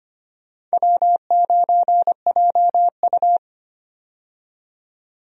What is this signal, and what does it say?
Morse code, heard as a keyed tone, about 25 wpm, 700 Hz. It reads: W9JU
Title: USA Call Signs - 4 Characters - 25wpm